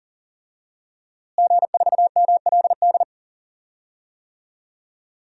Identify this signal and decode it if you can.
Morse code, heard as a keyed tone, about 40 wpm, 700 Hz. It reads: G4MLD
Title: DXCC-100 Call Signs - 5 Characters - Repeated 3x - 40wpm